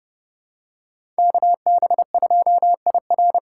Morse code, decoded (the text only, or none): KB2IR